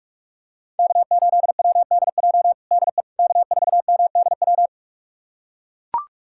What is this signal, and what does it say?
Morse code, heard as a keyed tone, about 45 wpm, 700 Hz. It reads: K8WDJ DE K4MDW
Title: USA Callsign DE Callsign - 5 Characters - 45wpm